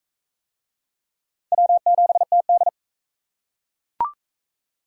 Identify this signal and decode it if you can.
Morse code, heard as a keyed tone, about 43 wpm, 700 Hz. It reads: W7TD